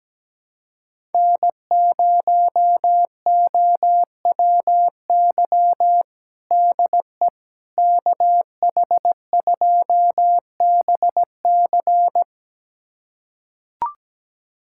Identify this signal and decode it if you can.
Morse code, heard as a keyed tone, about 17 wpm, 700 Hz. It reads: N0OWY DE KH2BC
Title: USA Callsign DE Callsign - 5 Characters - 17wpm